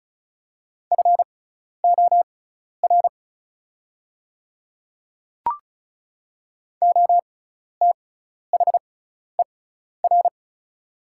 Morse code, heard as a keyed tone, about 34 wpm, spaced as for 12 wpm, 700 Hz. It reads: FOR OTHER
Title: 6x Character Spacing - Top 100 Words - 35wpm